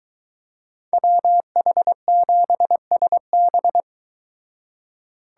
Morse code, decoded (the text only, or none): WH7SB